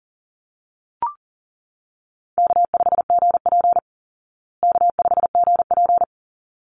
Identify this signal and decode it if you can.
Morse code, heard as a keyed tone, about 40 wpm, 700 Hz. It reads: K5GP K5GP
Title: Call Signs Repeated 3x - 40wpm